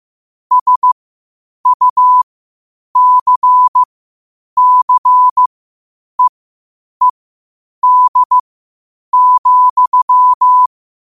Morse code, heard as a keyed tone, about 15 wpm, spaced as for 8 wpm, 1000 Hz. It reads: SUCCEED,
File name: qotdfiles / 8wpm